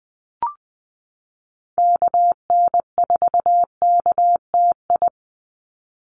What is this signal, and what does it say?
Morse code, heard as a keyed tone, about 20 wpm, 700 Hz. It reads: KN4KTI
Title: USA Call Signs 20wpm